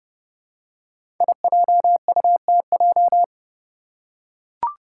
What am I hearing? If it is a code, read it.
Morse code, heard as a keyed tone, about 30 wpm, 700 Hz. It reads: IJUTJ
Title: Five Challenging Letter-Number Combinations 30wpm